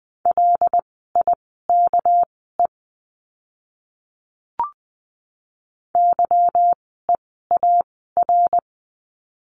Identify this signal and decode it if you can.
Morse code, heard as a keyed tone, about 20 wpm, spaced as for 15 wpm, 700 Hz. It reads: LIKE YEAR
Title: 230 Lesson 37f - Letter x - Words 20wpm